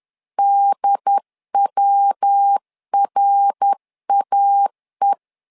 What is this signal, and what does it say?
Morse code, heard as a keyed tone, about 10 wpm, 800 Hz. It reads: DWRAE
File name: LEZIONI / 3 MWRO / 3 MWR 10wpm 1952